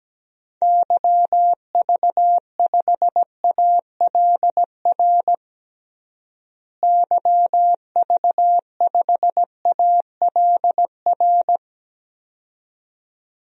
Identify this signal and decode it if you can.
Morse code, heard as a keyed tone, about 17 wpm, 700 Hz. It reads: YV5ALR YV5ALR
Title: DXCC-100 Call Signs - 6 Characters - Repeated 2x - 17wpm